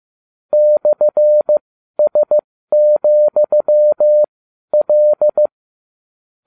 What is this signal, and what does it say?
Morse code, heard as a keyed tone, about 15 wpm, spaced as for 12 wpm, 600 Hz. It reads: /S,L